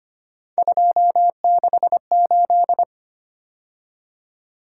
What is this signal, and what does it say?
Morse code, heard as a keyed tone, about 25 wpm, 700 Hz. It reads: 268
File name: Three Letter-Number Combinations - 25wpm